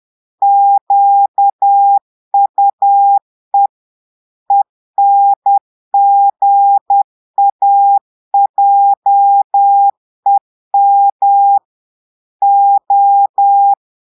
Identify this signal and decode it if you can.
Morse code, heard as a keyed tone, about 10 wpm, 800 Hz. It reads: QUE ENGAJEM O
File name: portugais texte en clair 2 vitesse 10 mots minutes